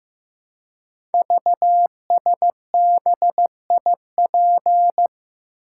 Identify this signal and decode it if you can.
Morse code, heard as a keyed tone, about 15 wpm, 700 Hz. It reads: VSBIP
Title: Code Groups - 5 Letter:Numbers - 15wpm